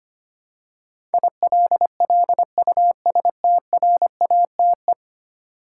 Morse code, heard as a keyed tone, about 25 wpm, 700 Hz. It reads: ILLUSTRATE
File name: Ten Letter Words - 25wpm